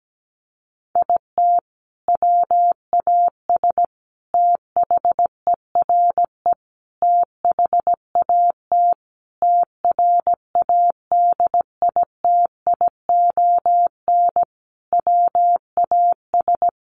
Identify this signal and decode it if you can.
Morse code, heard as a keyed tone, about 17 wpm, 700 Hz. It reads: IT WAS THERE THAT TRADITION WAS
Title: Sentences from Top 1500 Words 17wpm